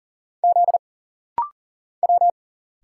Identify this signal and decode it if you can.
Morse code, heard as a keyed tone, about 44 wpm, 700 Hz. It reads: Z W
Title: Single Letter-Number Alt - Rapid-Fire 40wpm